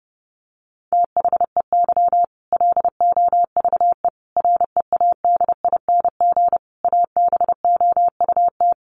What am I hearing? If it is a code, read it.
Morse code, heard as a keyed tone, about 28 wpm, 700 Hz. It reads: THEY LOVE READING ABOUT
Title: Sentences from Top 600 Words 30wpm